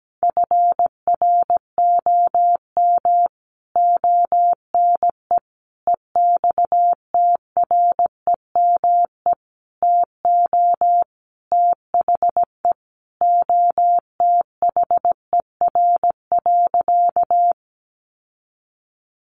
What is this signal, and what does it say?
Morse code, heard as a keyed tone, about 17 wpm, 700 Hz. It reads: FROM ONE EXTREME TO THE OTHER.
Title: Sentences from Top 1300 Words 17wpm